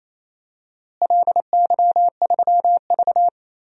Morse code, heard as a keyed tone, about 28 wpm, 700 Hz. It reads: LY3V
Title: DXCC-100 Call Signs - 4 Characters - Repeated 2x - 28wpm